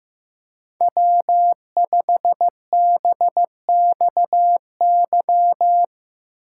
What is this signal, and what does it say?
Morse code, heard as a keyed tone, about 16 wpm, 700 Hz. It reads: W5BXY
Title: Speed Racing - USA 1x3 Call Signs 15wpm